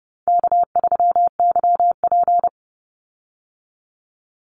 Morse code, heard as a keyed tone, about 30 wpm, 700 Hz. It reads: K3YP
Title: Call Signs Repeated 2x - 30wpm